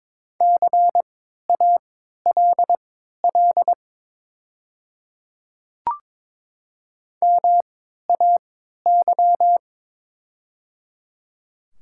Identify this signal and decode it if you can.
Morse code, heard as a keyed tone, about 22 wpm, spaced as for 13 wpm, 700 Hz. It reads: CALL MAY
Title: 3x Character Spacing - Top 300 Words - 22wpm